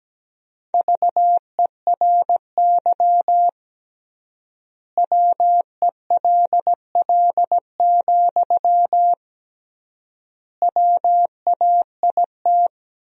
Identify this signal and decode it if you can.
Morse code, heard as a keyed tone, about 17 wpm, 700 Hz. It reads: VERY WELL, WAIT
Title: 3x Word-Spacing - Sentences from Top 500 Words - 17wpm